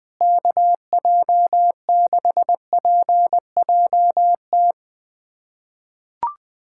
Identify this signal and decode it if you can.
Morse code, heard as a keyed tone, about 20 wpm, 700 Hz. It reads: KJ6PJT
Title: USA Callsign DE Callsign - 6 Characters - 20wpm